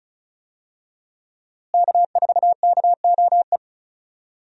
Morse code, heard as a keyed tone, about 34 wpm, 700 Hz. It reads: K4KOE
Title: USA Call Signs - 5 Characters - Repeated 3x - 35wpm